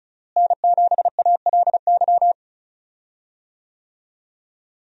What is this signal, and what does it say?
Morse code, heard as a keyed tone, about 34 wpm, 700 Hz. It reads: N7ALY
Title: USA Call Signs - 5 Characters - Repeated 2x - 35wpm